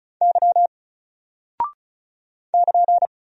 Morse code, heard as a keyed tone, about 36 wpm, 700 Hz. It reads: Y <KN>
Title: Single Character (Letter:Number:Punctuation:Prosign) 35wpm